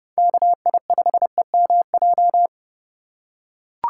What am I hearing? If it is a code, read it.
Morse code, heard as a keyed tone, about 30 wpm, 700 Hz. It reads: KI5EMJ
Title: USA Call Signs - 4x Character Spacing - 30wpm